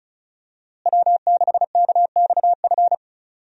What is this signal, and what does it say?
Morse code, heard as a keyed tone, about 34 wpm, 700 Hz. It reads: W6KXF
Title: USA Call Signs - 5 Characters - Repeated 2x - 35wpm